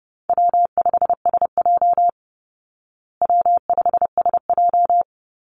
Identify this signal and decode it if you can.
Morse code, heard as a keyed tone, about 28 wpm, 700 Hz. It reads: W5SJ W5SJ
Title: Easy Call Signs Repeated 2x - 30wpm